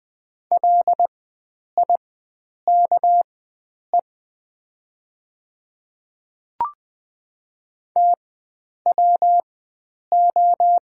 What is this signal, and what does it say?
Morse code, heard as a keyed tone, about 20 wpm, spaced as for 9 wpm, 700 Hz. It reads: LIKE TWO
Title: 4x Character Spacing - Top 100 Words - 20wpm